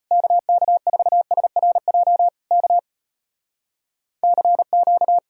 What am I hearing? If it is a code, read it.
Morse code, heard as a keyed tone, about 33 wpm, 700 Hz. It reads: KK4SRJ K CQ